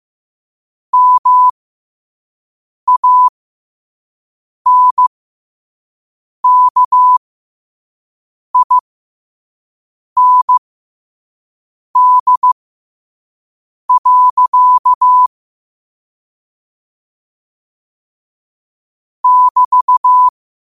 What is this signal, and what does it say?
Morse code, heard as a keyed tone, about 15 wpm, spaced as for 5 wpm, 1000 Hz. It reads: MANKIND. <BT>
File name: qotdfiles / 5wpm